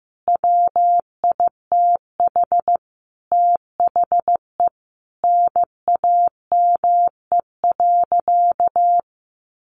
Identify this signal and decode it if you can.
Morse code, heard as a keyed tone, about 15 wpm, 700 Hz. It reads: WITH THE NAME.
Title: Sentences from Top 1100 Words 15wpm